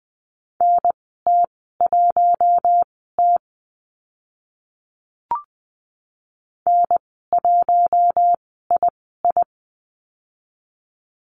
Morse code, heard as a keyed tone, about 20 wpm, spaced as for 15 wpm, 700 Hz. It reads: NT1T N1II
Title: 042 Lesson 8g - Number 1 - Call Signs 20wpm